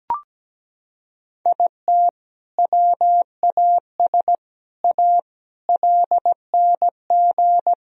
Morse code, heard as a keyed tone, about 17 wpm, 700 Hz. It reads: IT WAS A LNG
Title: Sentences from Top 500 Words and Abbreviations 17wpm